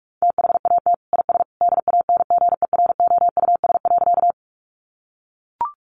Morse code, heard as a keyed tone, about 44 wpm, 700 Hz. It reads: THAT IS DANGEROUS.
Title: Sentences from Top 1200 Words 45wpm